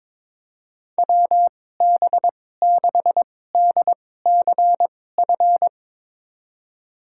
Morse code, heard as a keyed tone, about 21 wpm, spaced as for 16 wpm, 700 Hz. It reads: WB6DCF
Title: USA Call Signs - 6 Characters - 2x Character Spacing - 22wpm